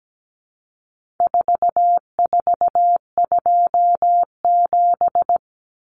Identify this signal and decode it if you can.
Morse code, heard as a keyed tone, about 17 wpm, 700 Hz. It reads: 4427
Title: Numbers - 4-Digit 17wpm